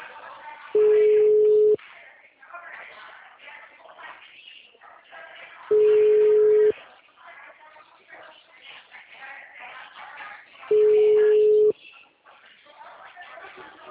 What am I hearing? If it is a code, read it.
Ringback tone (Europe, 425 Hz), with speech.